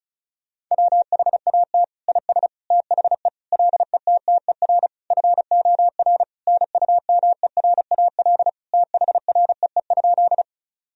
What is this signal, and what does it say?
Morse code, heard as a keyed tone, about 34 wpm, 700 Hz. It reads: WHAT IS THE LETTER FOR NUMERAL THREE?
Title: Sentences from Top 1000 Words - Review Part 2 - 35wpm